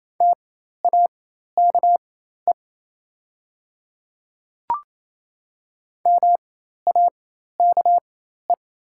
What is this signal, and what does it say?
Morse code, heard as a keyed tone, about 28 wpm, spaced as for 13 wpm, 700 Hz. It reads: TAKE MAKE
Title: 4x Character Spacing - Top 300 Words - 28wpm